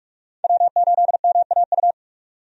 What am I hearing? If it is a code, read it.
Morse code, heard as a keyed tone, about 41 wpm, 700 Hz. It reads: W8MAU
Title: USA Call Signs - 5 Characters - 45wpm